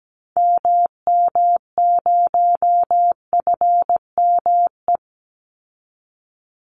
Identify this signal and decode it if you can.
Morse code, heard as a keyed tone, about 17 wpm, 700 Hz. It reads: MM0FME
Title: Call Signs - 17wpm